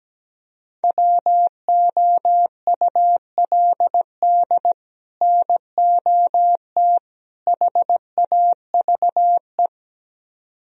Morse code, heard as a keyed tone, about 17 wpm, 700 Hz. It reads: WOULD NOT HAVE
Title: Speed-Racing - Sets of 3 Words 17wpm